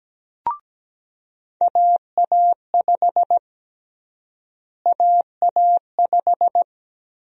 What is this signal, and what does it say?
Morse code, heard as a keyed tone, about 17 wpm, 700 Hz. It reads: AA5 AA5
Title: USA Call Sign Prefixes - Repeated 2x - 17wpm